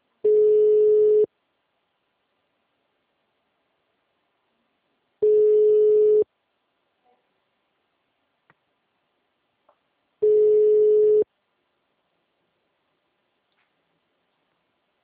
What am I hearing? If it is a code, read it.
Ringback tone (Europe, 425 Hz).
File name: out- -2002- - - .17238